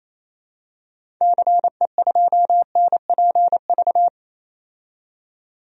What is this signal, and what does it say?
Morse code, heard as a keyed tone, about 28 wpm, 700 Hz. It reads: CE2NPV
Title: DXCC-100 Call Signs - 6 Characters - 28wpm